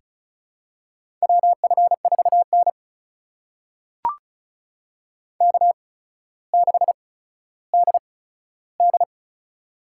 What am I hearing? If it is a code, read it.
Morse code, heard as a keyed tone, about 34 wpm, spaced as for 15 wpm, 700 Hz. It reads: WF4N K6DD
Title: USA Call Signs - 4 Characters - 8x Character Spacing - 35wpm